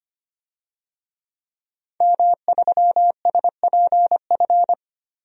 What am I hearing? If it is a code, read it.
Morse code, heard as a keyed tone, about 25 wpm, 700 Hz. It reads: M3SPF
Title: DXCC-100 Call Signs - 5 Characters - Repeated 3x - 25wpm